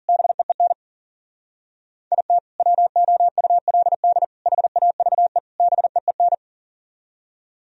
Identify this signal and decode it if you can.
Morse code, heard as a keyed tone, about 53 wpm, 700 Hz. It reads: BEEN IT WOULD HAVE BEEN